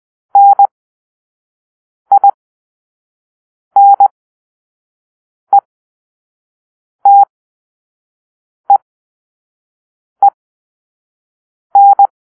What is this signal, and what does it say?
Morse code, heard as a keyed tone, about 20 wpm, spaced as for 5 wpm, 800 Hz. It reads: NINETEEN
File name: arrl / Grp-2 Drill-3